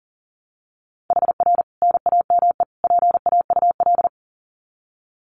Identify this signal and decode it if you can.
Morse code, heard as a keyed tone, about 40 wpm, 700 Hz. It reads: HR NAME PAUL
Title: Common QSO Phrases 40wpm